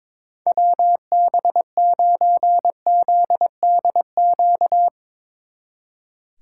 Morse code, heard as a keyed tone, about 22 wpm, 700 Hz. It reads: WB9ZDQ
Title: Speed Racing - USA Call Signs 22wpm